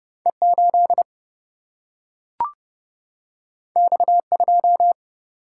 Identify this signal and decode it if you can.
Morse code, heard as a keyed tone, about 30 wpm, 700 Hz. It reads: E8 X2
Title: Single-Letter Single-Number 30wpm